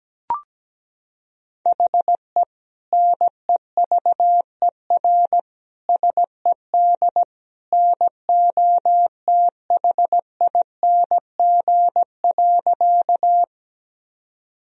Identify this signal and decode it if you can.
Morse code, heard as a keyed tone, about 17 wpm, 700 Hz. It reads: HE NEVER SED NOTHING.
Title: Sentences from Top 500 Words and Abbreviations 17wpm